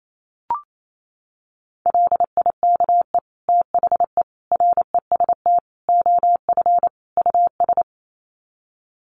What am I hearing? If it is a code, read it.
Morse code, heard as a keyed tone, about 28 wpm, 700 Hz. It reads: LIKE THE REST OF US
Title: Sets of 5 Words Encore2 28wpm